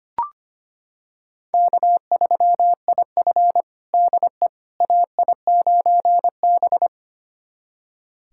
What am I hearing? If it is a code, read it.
Morse code, heard as a keyed tone, about 25 wpm, 700 Hz. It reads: K3IF DE AI9B
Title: USA Callsign DE Callsign - 4 Characters - 25wpm